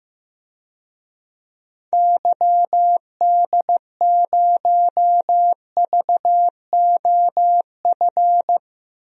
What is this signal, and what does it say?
Morse code, heard as a keyed tone, about 15 wpm, 700 Hz. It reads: YD0VOF
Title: DXCC-100 Call Signs - 6 Characters - 15wpm